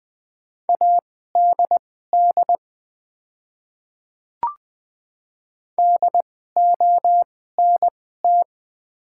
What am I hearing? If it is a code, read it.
Morse code, heard as a keyed tone, about 20 wpm, spaced as for 14 wpm, 700 Hz. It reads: ADD DONT
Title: 2x Character Spacing - Top 300 Words - 20wpm